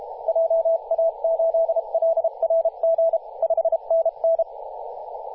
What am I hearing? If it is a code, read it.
Morse code, heard as a keyed tone, about 32 wpm, 660 Hz, with radio noise. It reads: JA8LRG 5NN
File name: posts / 2016-oct-05-2124z-t31t-18-072mhz-cw